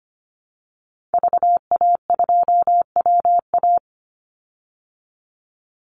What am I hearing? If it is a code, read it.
Morse code, heard as a keyed tone, about 25 wpm, 700 Hz. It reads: VA2WA
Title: Call Signs Repeated 3x - 25wpm